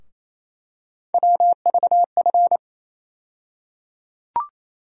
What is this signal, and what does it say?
Morse code, heard as a keyed tone, about 28 wpm, 700 Hz. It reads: WVF
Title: USA Call Sign Postfixes - 3 Letters 28wpm